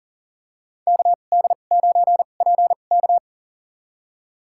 Morse code, heard as a keyed tone, about 40 wpm, spaced as for 29 wpm, 700 Hz. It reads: KD9PK
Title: USA Call Signs - 5 Characters - 2x Character Spacing - 40wpm